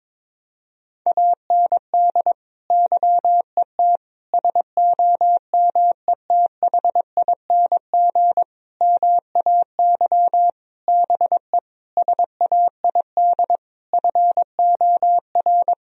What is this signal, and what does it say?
Morse code, heard as a keyed tone, about 22 wpm, 700 Hz. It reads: AND YET SOMETHING MAY BE SAID FOR
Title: Sentences from Top 1000 Words - Review Part 3 - 22wpm